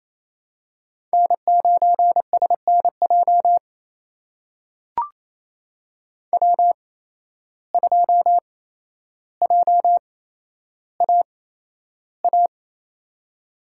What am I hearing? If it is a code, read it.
Morse code, heard as a keyed tone, about 28 wpm, spaced as for 12 wpm, 700 Hz. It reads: N9SNJ W2JAA
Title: USA Call Signs - 8x Character Spacing - 28wpm